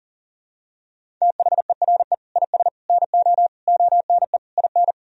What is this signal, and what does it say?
Morse code, heard as a keyed tone, about 40 wpm, 700 Hz. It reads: THERE IS NO ONE IN
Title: Sentences from Top 500 Words and Abbreviations 40wpm